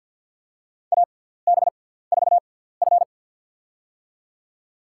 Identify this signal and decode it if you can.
Morse code, heard as a keyed tone, about 49 wpm, spaced as for 17 wpm, 700 Hz. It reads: AB4F
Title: USA Call Signs - 6x Character Spacing - 50wpm